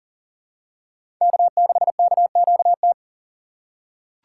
Morse code, heard as a keyed tone, about 40 wpm, 700 Hz. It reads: K6KQT
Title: USA Call Signs - 5 Characters - 6x Character Spacing - 40wpm